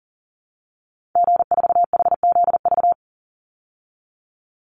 Morse code, heard as a keyed tone, about 40 wpm, 700 Hz. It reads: G4HZV
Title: Call Signs Repeated 3x - 40wpm